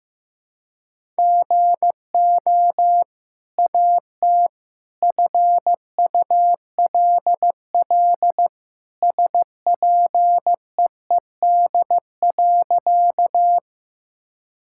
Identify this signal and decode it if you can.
Morse code, heard as a keyed tone, about 15 wpm, 700 Hz. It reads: GO AT FULL SPEED.
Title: Sentences from Top 1000 Words - Review Part 5 - 15wpm